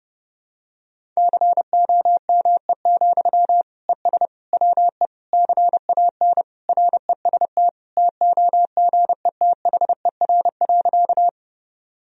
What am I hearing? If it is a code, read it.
Morse code, heard as a keyed tone, about 30 wpm, 700 Hz. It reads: COME, ES WE CAN REST TOGETHER.
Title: Sentences from Top 500 Words and Abbreviations 30wpm